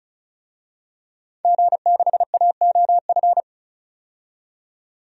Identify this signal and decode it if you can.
Morse code, heard as a keyed tone, about 34 wpm, 700 Hz. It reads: G6AOF
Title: DXCC-100 Call Signs - 5 Characters - 35wpm